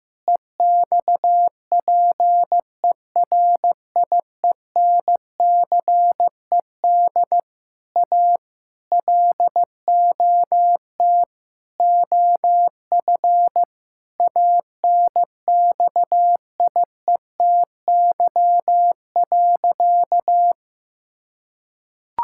Morse code, heard as a keyed tone, about 16 wpm, 700 Hz. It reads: EXPERIENCED A LOT OF ANXIETY.
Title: Sentences from Top 1600 Words 15wpm